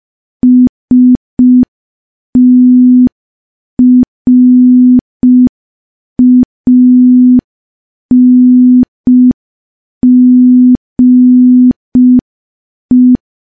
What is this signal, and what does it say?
Morse code, heard as a keyed tone, about 5 wpm, 250 Hz. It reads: STRANGE
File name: morse / morse-training-05WPM